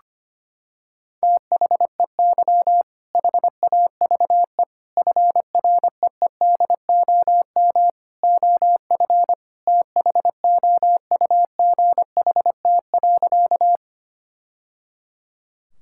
Morse code, heard as a keyed tone, about 25 wpm, 700 Hz. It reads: THEY HAVE FREEDOM OF THOUGHT.
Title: Sentences from Top 1000 Words - Review Part 5 - 25wpm